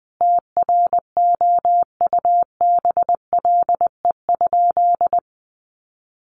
Morse code, heard as a keyed tone, about 19 wpm, 700 Hz. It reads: TROUBLE?
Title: Sentences from Top 1200 Words 20wpm